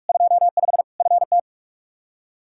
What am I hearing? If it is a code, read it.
Morse code, heard as a keyed tone, about 41 wpm, 700 Hz. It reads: 25 FT
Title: K1USN SST QSO Practice 45wpm